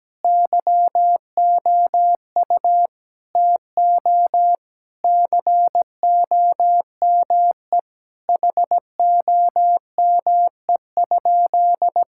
Morse code, heard as a keyed tone, about 17 wpm, 700 Hz. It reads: YOU TO COME HOME?
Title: Sentences from Top 1600 Words 17wpm